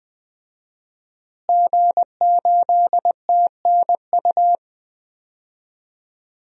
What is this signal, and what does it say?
Morse code, heard as a keyed tone, about 21 wpm, 700 Hz. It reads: G8TNU
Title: DXCC-100 Call Signs - 5 Characters - Repeated 3x - 20wpm